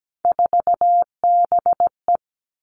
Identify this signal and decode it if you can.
Morse code, heard as a keyed tone, about 17 wpm, 700 Hz. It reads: 4BE
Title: Three Challenging Letter-Number Combinations 17wpm